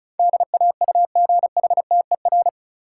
Morse code, heard as a keyed tone, about 36 wpm, 700 Hz. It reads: DAUGHTER
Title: Sentences from Top 1500 Words - New Word Drill Alt - 35wpm